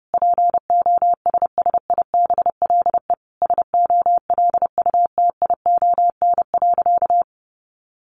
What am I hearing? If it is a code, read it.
Morse code, heard as a keyed tone, about 30 wpm, 700 Hz. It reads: POSSIBLE SOLUTION.
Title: Sentences from Top 1300 Words 30wpm